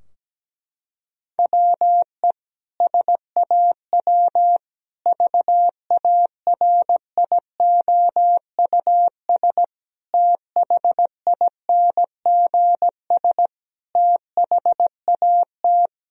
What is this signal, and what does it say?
Morse code, heard as a keyed tone, about 17 wpm, 700 Hz. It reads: WE SAW VARIOUS THINGS THAT
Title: Sentences from Top 1000 Words - Review Part 4 - 17wpm